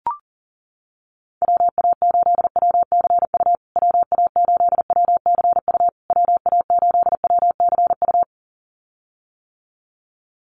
Morse code, heard as a keyed tone, about 40 wpm, 700 Hz. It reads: WA8WCU WA8WCU WA8WCU
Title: USA Call Signs Repeated 3x 40wpm